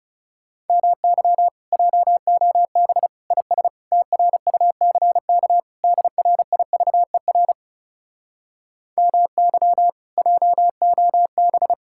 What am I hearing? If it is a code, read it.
Morse code, heard as a keyed tone, about 40 wpm, 700 Hz. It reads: MY JOB IS TRUCK DRIVER MY JOB